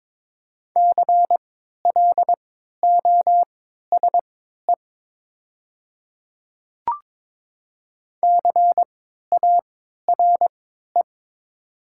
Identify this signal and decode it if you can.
Morse code, heard as a keyed tone, about 23 wpm, spaced as for 13 wpm, 700 Hz. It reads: CLOSE CARE
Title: 3x Character Spacing - Top 300 Words - 22wpm